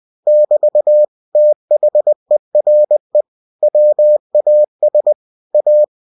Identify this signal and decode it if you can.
Morse code, heard as a keyed tone, about 20 wpm, 600 Hz. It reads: <BT> THERE WAS A
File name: cw-books / winnie-the-pooh / winnie-20wpm-page-0013